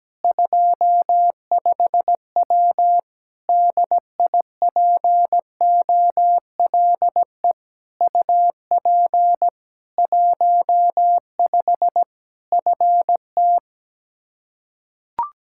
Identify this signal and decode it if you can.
Morse code, heard as a keyed tone, about 17 wpm, 700 Hz. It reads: 25W DIPOLE UP 15 FT
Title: Practice QSO - Exchange 2 - Info - 17wpm